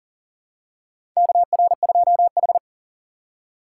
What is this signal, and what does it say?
Morse code, heard as a keyed tone, about 40 wpm, 700 Hz. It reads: KR2H
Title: USA Call Signs - 4 Characters - 2x Character Spacing - 40wpm